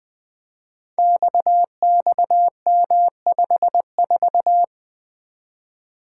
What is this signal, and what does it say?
Morse code, heard as a keyed tone, about 20 wpm, 700 Hz. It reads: XXM54
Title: Code Groups - 5 Letter:Numbers - 20wpm